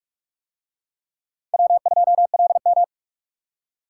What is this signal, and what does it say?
Morse code, heard as a keyed tone, about 44 wpm, 700 Hz. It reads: W2LM